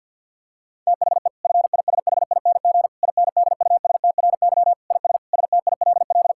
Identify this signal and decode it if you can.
Morse code, heard as a keyed tone, about 50 wpm, 700 Hz. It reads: THE FISHING INDUSTRY IS STILL
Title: Sentences from Top 1000 Words - Review Part 6 - 50wpm